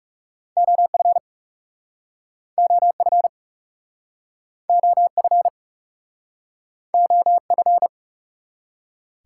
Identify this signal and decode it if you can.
Morse code, heard as a keyed tone, about 40 wpm, 700 Hz. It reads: OF OF OF OF